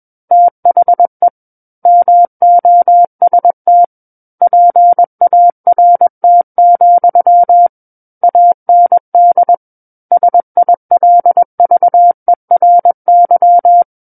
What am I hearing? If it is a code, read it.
Morse code, heard as a keyed tone, about 21 wpm, 700 Hz. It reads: THE MOST PART, AND SILVERY
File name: Book One / 12-Chapter 12 - 21 WPM